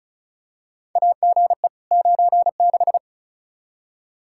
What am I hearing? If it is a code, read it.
Morse code, heard as a keyed tone, about 34 wpm, 700 Hz. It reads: AGE 96
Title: Practice QSO - Exchange 3 - More Info - 35wpm